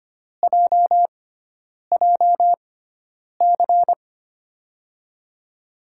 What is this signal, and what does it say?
Morse code, heard as a keyed tone, about 25 wpm, spaced as for 9 wpm, 700 Hz. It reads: JJC
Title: USA Call Signs - 5 Characters - 6x Character Spacing - 25wpm